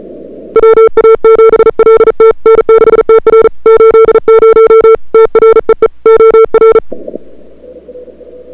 Morse code, heard as a keyed tone, about 37 wpm, 440 Hz. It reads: WA7LT N6TR 80 TREE OR
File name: CWSprint / February 2007 CW Sprint / 01-2007 CW Sprint Hour1